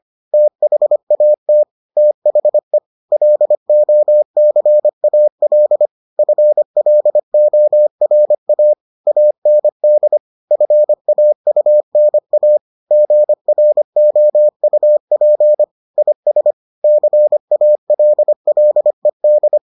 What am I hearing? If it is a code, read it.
Morse code, heard as a keyed tone, about 25 wpm, 600 Hz. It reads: THAT THE LOCAL FLORA AND FAUNA GROUP IS CALLED